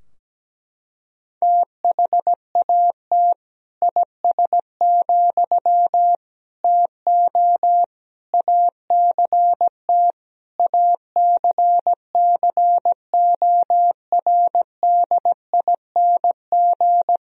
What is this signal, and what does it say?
Morse code, heard as a keyed tone, about 17 wpm, 700 Hz. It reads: THAT IS, TO ACT ACCORDING
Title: Sentences from Top 1000 Words - Review Part 4 - 17wpm